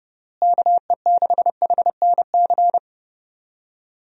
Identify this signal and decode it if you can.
Morse code, heard as a keyed tone, about 30 wpm, 700 Hz. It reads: KE6HNC
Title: USA Call Signs - 6 Characters - Repeated 2x - 30wpm